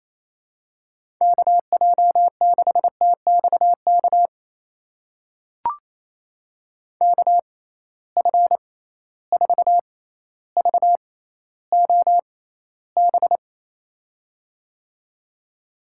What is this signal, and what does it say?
Morse code, heard as a keyed tone, about 28 wpm, spaced as for 14 wpm, 700 Hz. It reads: KJ6TXK KF4VOB
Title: USA Call Signs - 6 Characters - 6x Character Spacing - 28wpm